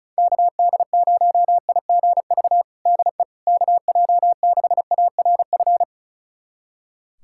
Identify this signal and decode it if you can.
Morse code, heard as a keyed tone, about 36 wpm, 690 Hz. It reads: KD0IGV DE KJ6ARF
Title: USA Callsign DE Callsign - 6 Characters - 35wpm